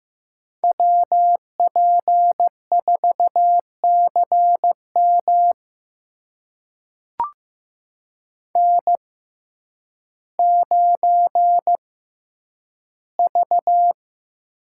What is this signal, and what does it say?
Morse code, heard as a keyed tone, about 15 wpm, spaced as for 9 wpm, 700 Hz. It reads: WP4CM N9V
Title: USA Call Signs - 5 Characters - 6x Character Spacing - 15wpm